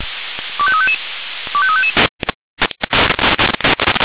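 Five-tone selective call (ZVEI), heard as quick runs of five tones, tones of about 70 ms. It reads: 25355 25355
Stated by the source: ZVEI five-tone selective call received on an SDR.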